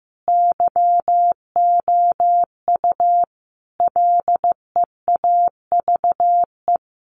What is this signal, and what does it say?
Morse code, heard as a keyed tone, about 15 wpm, 700 Hz. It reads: YOU LEAVE
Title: A Sound Of Thunder 15wpm